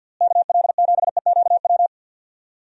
Morse code, heard as a keyed tone, about 52 wpm, 700 Hz. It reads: KL7EQW
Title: DXCC-100 Call Signs - 6 Characters - Repeated 3x - 50wpm